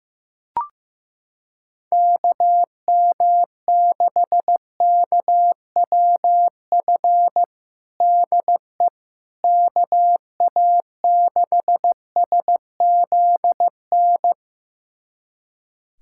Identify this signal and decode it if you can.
Morse code, heard as a keyed tone, about 15 wpm, 700 Hz. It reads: KM6KWF DE KA6SZN
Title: USA Callsign DE Callsign - 6 Characters - 15wpm